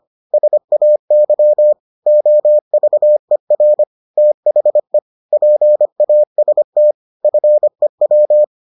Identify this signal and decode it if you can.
Morse code, heard as a keyed tone, about 25 wpm, 600 Hz. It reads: SAY OVER THE PAST FEW